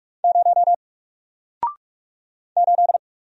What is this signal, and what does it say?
Morse code, heard as a keyed tone, about 45 wpm, 700 Hz. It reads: 0 8
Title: Single Letter-Number Alt 45wpm